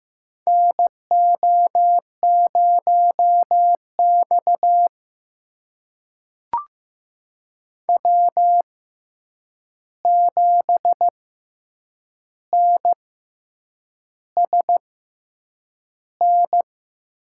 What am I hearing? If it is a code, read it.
Morse code, heard as a keyed tone, about 15 wpm, spaced as for 7 wpm, 700 Hz. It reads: NO0X W7NSN
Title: USA Call Signs - 6x Character Spacing - 15wpm